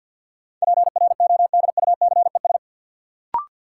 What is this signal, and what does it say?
Morse code, heard as a keyed tone, about 53 wpm, 700 Hz. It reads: PRODUCES
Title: Sentences from Top 1000 Words - Review Part 6 - New Word Drill - 50wpm